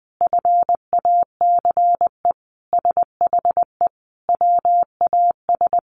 Morse code, heard as a keyed tone, about 20 wpm, 700 Hz. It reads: FACE SHE WAS
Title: Sentences from Top 1200 Words 20wpm